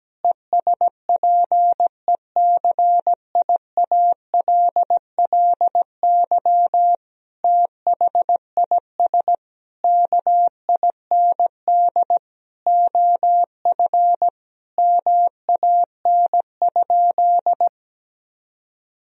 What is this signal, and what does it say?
Morse code, heard as a keyed tone, about 17 wpm, 700 Hz. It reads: ESPECIALLY THIS KIND OF MAN?
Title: Sentences from Top 1000 Words - Review Part 4 - 17wpm